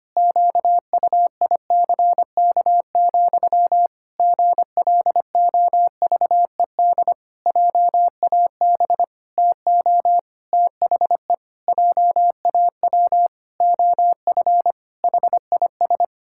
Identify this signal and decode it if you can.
Morse code, heard as a keyed tone, about 25 wpm, 700 Hz. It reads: QUICK, GLOVED JAB TO THE JAW OF HIS
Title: Pangram Challenge v2 - 25wpm